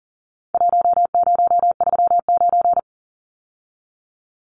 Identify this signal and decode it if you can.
Morse code, heard as a keyed tone, about 40 wpm, 700 Hz. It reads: 1039
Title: Numbers - Contest Style - 4-Digit 40wpm